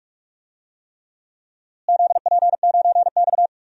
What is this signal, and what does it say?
Morse code, heard as a keyed tone, about 44 wpm, 700 Hz. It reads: ZP9X
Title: DXCC-100 Call Signs - 4 Characters - Repeated 3x - 45wpm